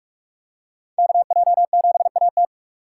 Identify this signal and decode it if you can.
Morse code, heard as a keyed tone, about 44 wpm, 700 Hz. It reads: KJ7AT